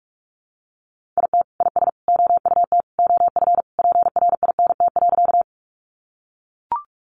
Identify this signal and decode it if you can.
Morse code, heard as a keyed tone, about 44 wpm, 700 Hz. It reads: IT IS OUT OF PRINT.
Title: Sentences from Top 1400 Words 45wpm